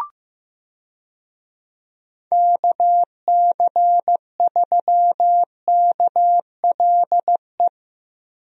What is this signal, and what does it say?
Morse code, heard as a keyed tone, about 15 wpm, 700 Hz. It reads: KC3KLE
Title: USA Call Signs - 6 Characters - Repeated 3x - 15wpm